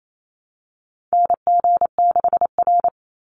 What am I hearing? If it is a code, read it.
Morse code, heard as a keyed tone, about 28 wpm, 700 Hz. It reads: NG6R
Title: NA SOTA Chaser Callsigns 28wpm